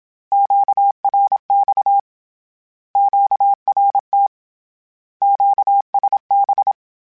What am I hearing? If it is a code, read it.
Morse code, heard as a keyed tone, about 27 wpm, 800 Hz. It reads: QRX QRT QSB
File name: Short Runs / Short Runs-mp3s / Words @25